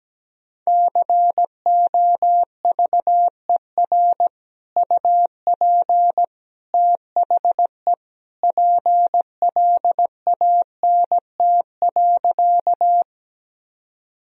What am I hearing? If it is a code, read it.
Morse code, heard as a keyed tone, about 17 wpm, 700 Hz. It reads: COVER UP THE PLANT.
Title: Sentences from Top 1000 Words - Review Part 1 - 17wpm